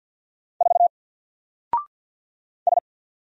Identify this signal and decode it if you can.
Morse code, heard as a keyed tone, about 48 wpm, 700 Hz. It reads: 4 S
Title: Single Character (Letter:Number:Punctuation:Prosign) 50wpm